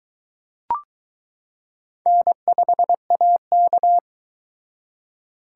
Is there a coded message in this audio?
Yes: Morse code from 2.1 to 4.0 s, heard as a keyed tone, about 22 wpm, 700 Hz. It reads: N5AK